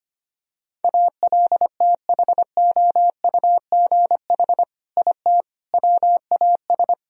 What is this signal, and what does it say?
Morse code, heard as a keyed tone, about 25 wpm, 700 Hz. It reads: ALTHOUGH IT WAS
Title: Sentences from Top 1000 Words - Review Part 3 - 25wpm